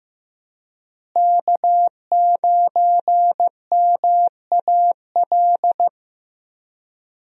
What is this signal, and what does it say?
Morse code, heard as a keyed tone, about 15 wpm, 700 Hz. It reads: K9MAL
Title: USA Call Signs - 5 Characters - Repeated 3x - 15wpm